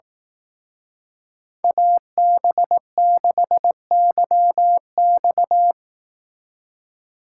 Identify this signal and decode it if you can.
Morse code, heard as a keyed tone, about 18 wpm, 700 Hz. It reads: AB6YX
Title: Speed Racing - USA 2x2 Call Signs 15wpm